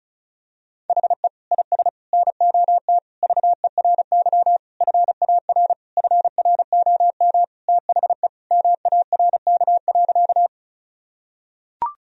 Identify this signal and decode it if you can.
Morse code, heard as a keyed tone, about 36 wpm, 700 Hz. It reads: HE IS NOT VERY FAR FROM THE MARK.
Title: Sentences from Top 1000 Words - Review Part 1 - 35wpm